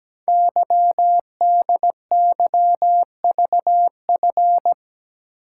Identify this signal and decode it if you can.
Morse code, heard as a keyed tone, about 17 wpm, 700 Hz. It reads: YDYVF
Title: Code Groups - 5 Letters - 17wpm